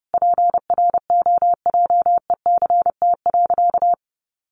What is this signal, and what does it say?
Morse code, heard as a keyed tone, about 30 wpm, 700 Hz. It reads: PROJECT.
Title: Sentences from Top 1400 Words 30wpm